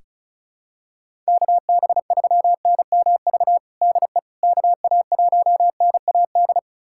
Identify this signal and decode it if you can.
Morse code, heard as a keyed tone, about 36 wpm, 700 Hz. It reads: KB3NMV DE KA1NAD
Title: USA Callsign DE Callsign - 6 Characters - 35wpm